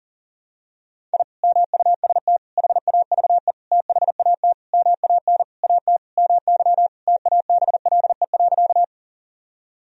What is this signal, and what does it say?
Morse code, heard as a keyed tone, about 40 wpm, 700 Hz. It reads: I MUST HAVE THAT MAN AT MY TABLE.
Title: Sentences from Top 1000 Words - Review Part 2 - 40wpm